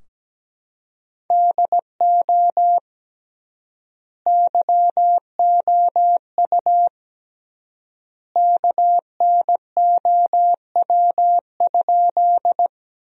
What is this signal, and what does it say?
Morse code, heard as a keyed tone, about 17 wpm, 700 Hz. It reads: DO YOU KNOW?
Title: 3x Word-Spacing - Sentences from Top 100 Words - 17wpm